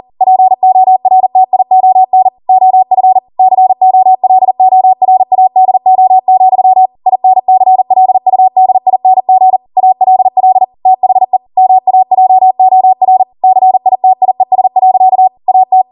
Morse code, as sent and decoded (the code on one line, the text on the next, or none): .--. --- .-. - .. --- -. / --- ..-. / -.-. --- .-.. --- .-. .- -.. --- --..-- / .. -. -.-. .-.. ..- -.. .. -. --. / .- .-.. .-.. / - .... / -- .- .--- --- .-. / -.-. .. - .. ... .-.-.- / .- -
PORTION OF COLORADO, INCLUDING ALL THE MAJOR CITIES. AT